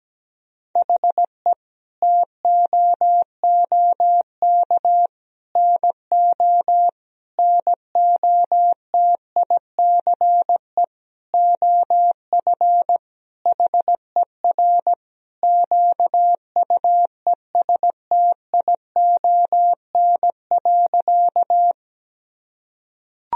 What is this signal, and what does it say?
Morse code, heard as a keyed tone, about 17 wpm, 700 Hz. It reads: HE TOOK NO NOTICE OF HER QUESTION.
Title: Sentences from Top 1000 Words - Review Part 2 - 17wpm